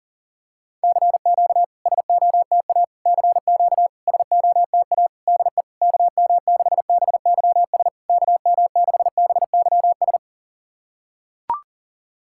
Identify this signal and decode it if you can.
Morse code, heard as a keyed tone, about 39 wpm, 690 Hz. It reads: CQ SOTA CQ SOTA DE KM6BYS KM6BYS
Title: SOTA QSO Practice 40wpm